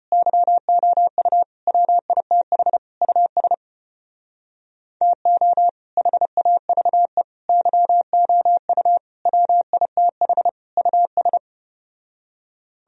Morse code, heard as a keyed tone, about 33 wpm, 700 Hz. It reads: YOU WITH US TO HAVE YOU WITH US